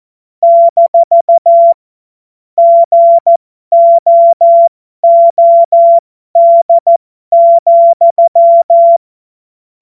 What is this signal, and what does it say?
Morse code, heard as a keyed tone, about 14 wpm, spaced as for 12 wpm, 680 Hz. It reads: - GOOD,
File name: Morsekurs / Witze